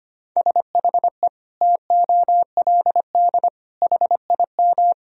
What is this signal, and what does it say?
Morse code, heard as a keyed tone, about 25 wpm, 700 Hz. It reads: SHE TOLD HIM
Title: Sentences from Top 1000 Words - Review Part 1 - 25wpm